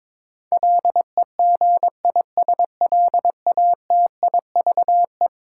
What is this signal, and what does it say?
Morse code, heard as a keyed tone, about 22 wpm, 700 Hz. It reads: LEGISLATIVE
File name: practice / 22wpm / Eleven Letter Words - 22wpm